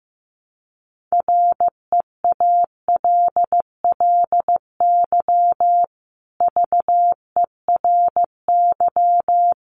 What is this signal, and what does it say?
Morse code, heard as a keyed tone, about 15 wpm, 700 Hz. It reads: REALLY VERY
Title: Sentences from Top 600 Words 15wpm